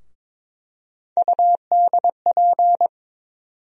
Morse code, heard as a keyed tone, about 22 wpm, 700 Hz. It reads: UDP
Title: USA Call Sign Postfixes - 3 Letters 22wpm